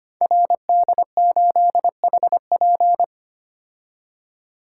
Morse code, heard as a keyed tone, about 25 wpm, 700 Hz. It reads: RD8HP
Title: Code Groups - 5 Letter:Numbers - 25wpm